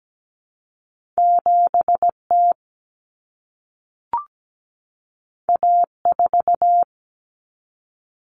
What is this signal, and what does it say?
Morse code, heard as a keyed tone, about 17 wpm, 700 Hz. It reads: 7T A4
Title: Common Cut Numbers - 2-Digit 17wpm